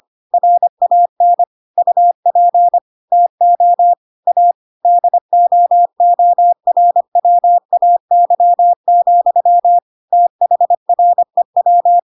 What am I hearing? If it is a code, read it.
Morse code, heard as a keyed tone, about 25 wpm, 700 Hz. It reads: RAN UP TO A DOORWAY, THREW